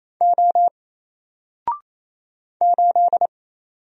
Morse code, heard as a keyed tone, about 27 wpm, 700 Hz. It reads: O 8
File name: Single-Letter-Number-28wpm